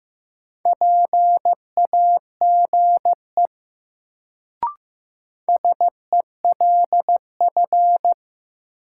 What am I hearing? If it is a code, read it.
Morse code, heard as a keyed tone, about 15 wpm, 700 Hz. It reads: PAGE SELF
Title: Sentences from Top 1000 Words - Review Part 1 - New Word Drill - 15wpm